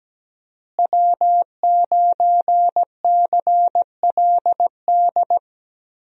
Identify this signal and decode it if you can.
Morse code, heard as a keyed tone, about 17 wpm, 700 Hz. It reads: W9CLD
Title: USA Call Signs - 5 Characters - Repeated 2x - 17wpm